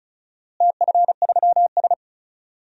Morse code, heard as a keyed tone, about 36 wpm, 700 Hz. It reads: TF3S
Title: DXCC-100 Call Signs - 4 Characters - Repeated 2x - 35wpm